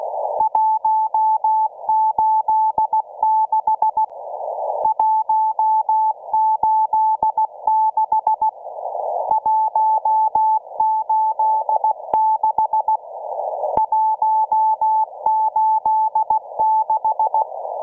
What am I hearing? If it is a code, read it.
Morse code, heard as a keyed tone, about 16 wpm, 870 Hz. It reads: 186 186 186 186